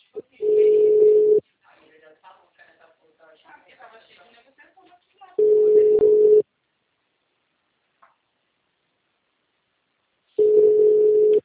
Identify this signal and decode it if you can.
Ringback tone (Europe, 425 Hz), with speech.